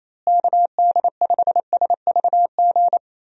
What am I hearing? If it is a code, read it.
Morse code, heard as a keyed tone, about 28 wpm, 700 Hz. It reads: KD5SVG
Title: USA Call Signs - 6 Characters - Repeated 2x - 28wpm